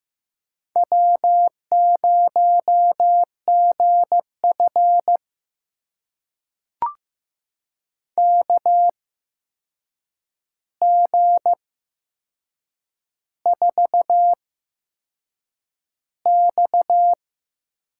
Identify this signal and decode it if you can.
Morse code, heard as a keyed tone, about 15 wpm, spaced as for 6 wpm, 700 Hz. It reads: W0GF KG4X
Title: USA Call Signs - 4 Characters - 8x Character Spacing - 15wpm